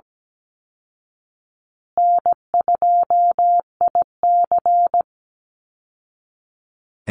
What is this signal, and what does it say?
Morse code, heard as a keyed tone, about 17 wpm, 700 Hz. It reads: N2IC
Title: Call Signs Repeated 2x - 17wpm